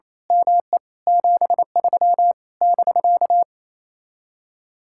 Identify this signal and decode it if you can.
Morse code, heard as a keyed tone, about 26 wpm, 700 Hz. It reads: ME 73 <BK>
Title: Speed-Racing Common QSO Phrases - 28wpm